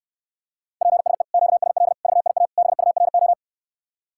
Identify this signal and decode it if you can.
Morse code, heard as a keyed tone, about 69 wpm, 700 Hz. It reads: PSE QSL VIA BURO